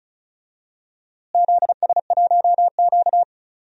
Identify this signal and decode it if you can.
Morse code, heard as a keyed tone, about 34 wpm, 700 Hz. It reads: ZS1Q
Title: DXCC-100 Call Signs - 4 Characters - Repeated 3x - 35wpm